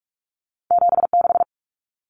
Morse code, heard as a keyed tone, about 46 wpm, 700 Hz. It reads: 76
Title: Common Cut Numbers - 2-Digit 45wpm